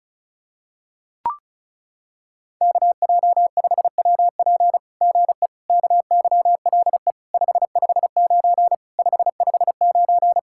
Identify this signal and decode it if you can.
Morse code, heard as a keyed tone, about 36 wpm, 700 Hz. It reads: KJ5WP GE KYLE 559 559
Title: SOTA QSO Practice 35wpm